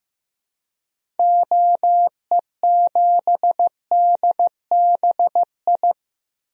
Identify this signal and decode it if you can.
Morse code, heard as a keyed tone, about 15 wpm, 700 Hz. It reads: OE7DBI
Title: DXCC-100 Call Signs - 6 Characters - Repeated 3x - 15wpm